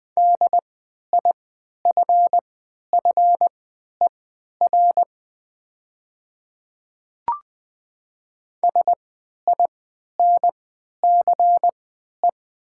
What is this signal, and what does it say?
Morse code, heard as a keyed tone, about 20 wpm, spaced as for 11 wpm, 700 Hz. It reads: DIFFER SINCE
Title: 3x Character Spacing - Top 500 Words - 20wpm